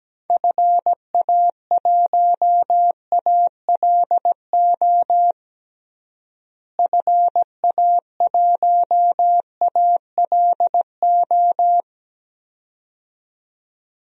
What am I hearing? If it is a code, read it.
Morse code, heard as a keyed tone, about 17 wpm, 700 Hz. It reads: FA1ALO FA1ALO
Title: DXCC-100 Call Signs - 6 Characters - Repeated 2x - 17wpm